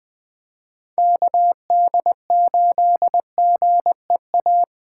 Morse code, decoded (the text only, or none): KD8GEA